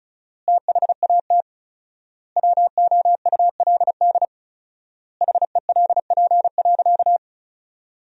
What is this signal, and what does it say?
Morse code, heard as a keyed tone, about 36 wpm, 700 Hz. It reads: THAT WOULD HELP.
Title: 4x Word-Spacing - Sentences from Top 200 Words - 35wpm